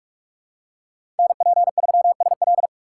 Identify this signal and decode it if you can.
Morse code, heard as a keyed tone, about 44 wpm, 700 Hz. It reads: NP3SL
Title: USA Call Signs - 5 Characters - 3x Character Spacing - 45wpm